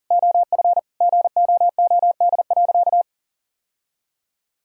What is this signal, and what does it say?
Morse code, heard as a keyed tone, about 37 wpm, 700 Hz. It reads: OF GOOD.
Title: Sentences from Top 1000 Words - Review Part 3 - 40wpm